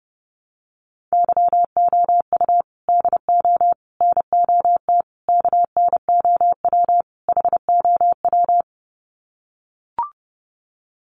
Morse code, heard as a keyed tone, about 30 wpm, 700 Hz. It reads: YOU DO NOT KNOW HOW
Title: Sets of 5 Words Encore 30wpm